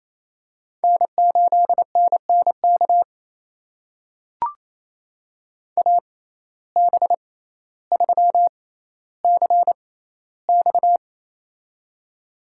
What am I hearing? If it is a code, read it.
Morse code, heard as a keyed tone, about 28 wpm, spaced as for 14 wpm, 700 Hz. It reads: N8NNK AB3CX
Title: USA Call Signs - 5 Characters - 6x Character Spacing - 28wpm